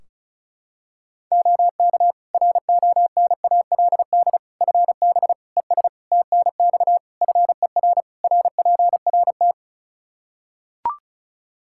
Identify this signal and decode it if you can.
Morse code, heard as a keyed tone, about 36 wpm, 700 Hz. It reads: OK RONALD FB ES TNX FER RPRT
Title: Practice QSO - Exchange 2 - Info - 35wpm